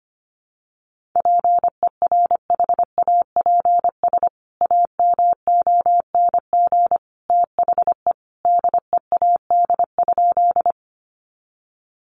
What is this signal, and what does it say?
Morse code, heard as a keyed tone, about 25 wpm, 700 Hz. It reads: PERHAPS AMONG THE DEAD?
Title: Sentences from Top 1500 Words 25wpm